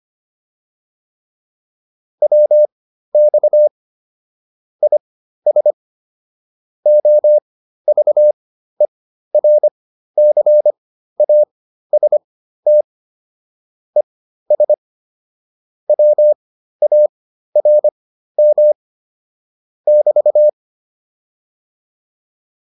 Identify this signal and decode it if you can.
Morse code, heard as a keyed tone, about 25 wpm, spaced as for 13 wpm, 600 Hz. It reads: WX IS OVERCAST ES WARM <BT>
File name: QSO301 13